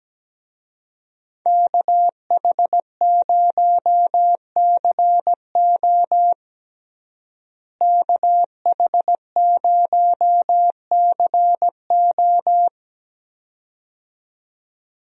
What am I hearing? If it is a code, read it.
Morse code, heard as a keyed tone, about 17 wpm, 700 Hz. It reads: KH0CO KH0CO
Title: USA Call Signs - 5 Characters - Repeated 2x - 17wpm